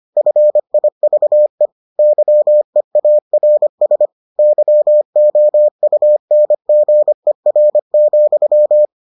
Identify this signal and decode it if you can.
Morse code, heard as a keyed tone, about 25 wpm, 600 Hz. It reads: FIVE YEARS YOUNGER,